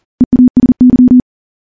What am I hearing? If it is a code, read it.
Morse code, heard as a keyed tone, about 40 wpm, 250 Hz. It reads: EASY
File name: morse / morse-training-40WPM